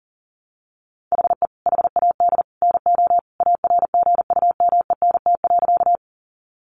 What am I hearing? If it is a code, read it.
Morse code, heard as a keyed tone, about 40 wpm, 700 Hz. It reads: HE HAD NO ARGUMENT.
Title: Sentences from Top 1400 Words 40wpm